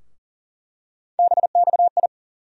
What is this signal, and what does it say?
Morse code, heard as a keyed tone, about 40 wpm, 700 Hz. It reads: BXI
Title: USA Call Sign Postfixes - 3 Letters 40wpm